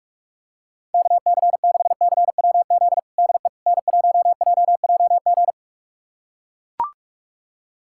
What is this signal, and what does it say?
Morse code, heard as a keyed tone, about 42 wpm, 700 Hz. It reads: KC6CWZ DE N1JJG
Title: USA Callsign DE Callsign 45wpm